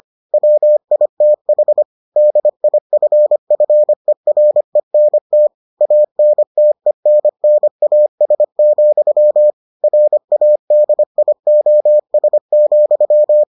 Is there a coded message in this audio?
Yes: Morse code from 0.3 to 13.4 s, heard as a keyed tone, about 25 wpm, 600 Hz. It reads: WITH DIFFERENT ANTENNAS, RADIOS,